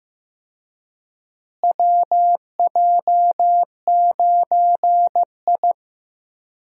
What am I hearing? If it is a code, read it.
Morse code, heard as a keyed tone, about 15 wpm, 700 Hz. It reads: WJ9I
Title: USA Call Signs - 4 Characters - Repeated 3x - 15wpm